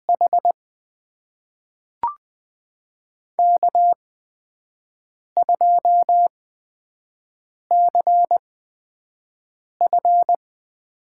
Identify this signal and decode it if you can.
Morse code, heard as a keyed tone, about 20 wpm, spaced as for 5 wpm, 700 Hz. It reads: H K2CF
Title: USA Call Signs - 4 Characters - 8x Character Spacing - 20wpm